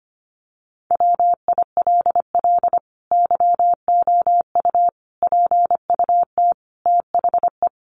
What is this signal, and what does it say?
Morse code, heard as a keyed tone, about 25 wpm, 700 Hz. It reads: WILL YOU PUT THE
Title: Sentences from Top 1400 Words 25wpm